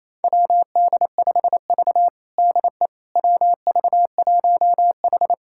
Morse code, heard as a keyed tone, about 28 wpm, 700 Hz. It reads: WD5V DE WV1H
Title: USA Callsign DE Callsign - 4 Characters - 28wpm